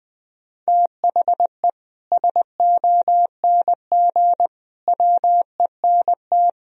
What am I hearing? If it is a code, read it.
Morse code, heard as a keyed tone, about 20 wpm, 700 Hz. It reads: THE SONG WENT
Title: Sentences from Top 1000 Words - Review Part 2 - 20wpm